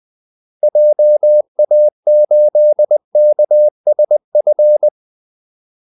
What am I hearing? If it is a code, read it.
Morse code, heard as a keyed tone, about 20 wpm, 600 Hz. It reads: JA8KSF
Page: Kaldesignaler 20 wpm